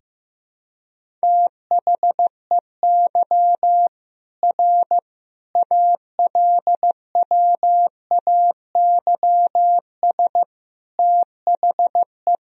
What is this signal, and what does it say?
Morse code, heard as a keyed tone, about 16 wpm, 700 Hz. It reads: THEY R ALWAYS THE
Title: Sentences from Top 300 Words and Abbreviations 15wpm